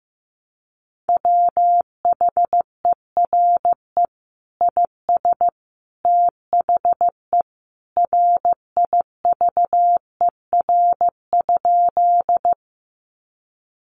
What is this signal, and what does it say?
Morse code, heard as a keyed tone, about 15 wpm, 700 Hz. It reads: WHERE IS THE RIVER?
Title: Sentences from Top 300 Words 15wpm